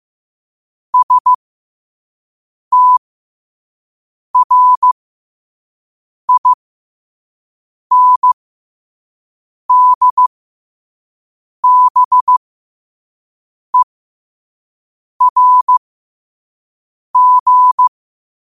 Morse code, heard as a keyed tone, about 15 wpm, spaced as for 5 wpm, 1000 Hz. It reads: STRINDBERG
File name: qotdfiles / 5wpm